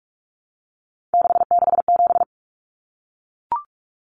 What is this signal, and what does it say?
Morse code, heard as a keyed tone, about 44 wpm, 700 Hz. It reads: 667
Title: Common Cut Numbers - 3-Digit 45wpm